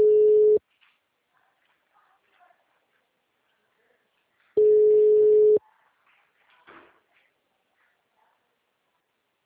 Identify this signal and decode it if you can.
Ringback tone (Europe, 425 Hz).